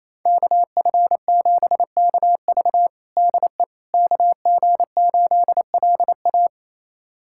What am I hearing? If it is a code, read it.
Morse code, heard as a keyed tone, about 28 wpm, 700 Hz. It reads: KF7KV DE KG8LA
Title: USA Callsign DE Callsign - 5 Characters - 28wpm